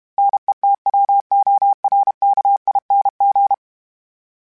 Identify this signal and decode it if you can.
Morse code, heard as a keyed tone, about 32 wpm, 800 Hz. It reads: NETWORKING
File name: Special INGs-mp3s / Special INGs-mp3s / Sngl Ing @30